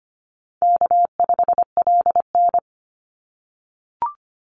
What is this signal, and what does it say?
Morse code, heard as a keyed tone, about 25 wpm, 700 Hz. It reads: K5LN
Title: Easy Call Signs - 25wpm